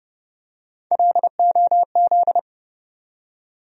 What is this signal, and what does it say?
Morse code, heard as a keyed tone, about 30 wpm, 700 Hz. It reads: LOZ
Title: USA Call Sign Postfixes - 3 Letters 30wpm